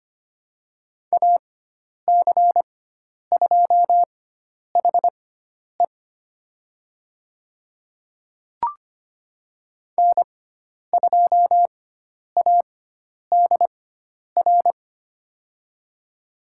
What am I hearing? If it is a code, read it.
Morse code, heard as a keyed tone, about 25 wpm, spaced as for 10 wpm, 700 Hz. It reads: AC2HE N2ADR
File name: Lesson 16g - Letter c - Call Signs 25wpm